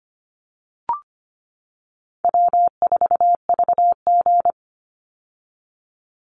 Morse code, heard as a keyed tone, about 25 wpm, 700 Hz. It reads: W4VG
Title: Easy Call Signs - 25wpm